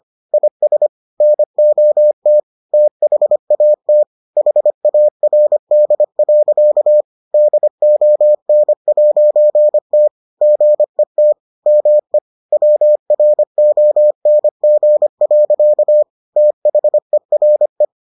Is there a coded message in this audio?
Yes: Morse code from 0.3 to 17.9 s, heard as a keyed tone, about 25 wpm, 600 Hz. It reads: IS NOT THAT HARD. DON'T GET ME WRONG. THERE